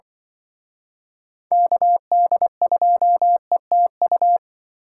Morse code, heard as a keyed tone, about 24 wpm, 700 Hz. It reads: KD2ETU
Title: Speed Racing - USA 2x3 Call Signs 20wpm